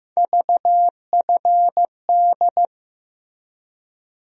Morse code, heard as a keyed tone, about 16 wpm, 700 Hz. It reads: VFD
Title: USA Call Sign Postfixes - 3 Letters 15wpm